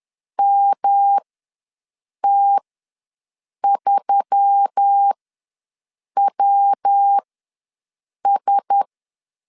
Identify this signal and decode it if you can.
Morse code, heard as a keyed tone, about 10 wpm, spaced as for 6 wpm, 800 Hz. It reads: MT3WS
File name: LEZIONI / 5 LBVC / 5 LBVC ws 10wpm 3733